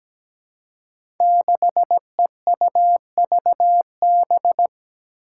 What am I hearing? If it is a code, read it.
Morse code, heard as a keyed tone, about 17 wpm, 700 Hz. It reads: 6EUVB
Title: Five Challenging Letter-Number Combinations 17wpm